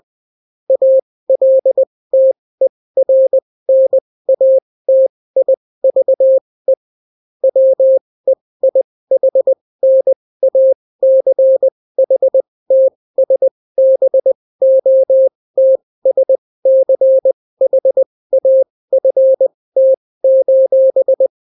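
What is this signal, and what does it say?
Morse code, heard as a keyed tone, about 20 wpm, spaced as for 16 wpm, 530 Hz. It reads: ALTERNATIVE WEIHNACHTSBOTSCHAFT:
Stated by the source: Daily Morse weather and top stories at 16 wpm, clean without QRN.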